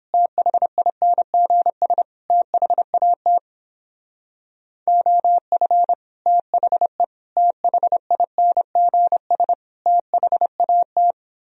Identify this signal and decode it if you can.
Morse code, heard as a keyed tone, about 29 wpm, 700 Hz. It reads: THINGS THAT OF THE THINGS THAT